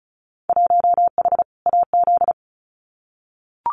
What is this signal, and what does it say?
Morse code, heard as a keyed tone, about 36 wpm, 700 Hz. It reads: 1H AZ
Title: Winter Field Day Exchanges 35wpm